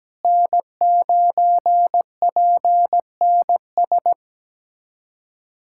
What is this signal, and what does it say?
Morse code, heard as a keyed tone, about 17 wpm, 700 Hz. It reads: N9PNS
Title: USA Callsign DE Callsign - 5 Characters - 17wpm